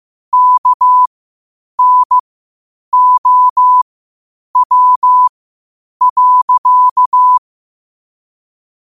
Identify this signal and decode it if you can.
Morse code, heard as a keyed tone, about 15 wpm, spaced as for 8 wpm, 1000 Hz. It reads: KNOW.
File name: qotdfiles / 8wpm